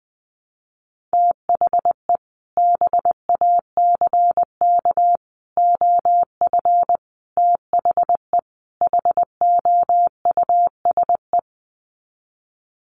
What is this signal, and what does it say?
Morse code, heard as a keyed tone, about 20 wpm, 700 Hz. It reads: THE BACK OF THE HOUSE
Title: Sets of 5 Words Encore2 20wpm